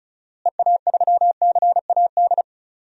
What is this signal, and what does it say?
Morse code, heard as a keyed tone, about 36 wpm, 700 Hz. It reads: EA3CAD
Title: DXCC-100 Call Signs - 6 Characters - Repeated 3x - 35wpm